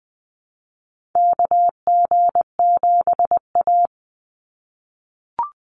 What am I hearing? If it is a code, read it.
Morse code, heard as a keyed tone, about 20 wpm, 700 Hz. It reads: KG7A
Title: Call Signs - 20wpm